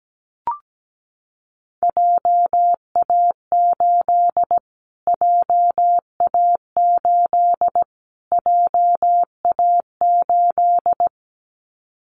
Morse code, heard as a keyed tone, about 17 wpm, 700 Hz. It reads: JA8 JA8 JA8
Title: Callsign Prefixes for Top 25 DXCC - Repeated 3x - 17wpm